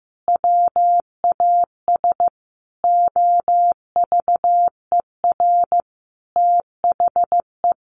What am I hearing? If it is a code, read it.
Morse code, heard as a keyed tone, about 15 wpm, 700 Hz. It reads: WAS OVER THE
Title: Sentences from Top 1100 Words 15wpm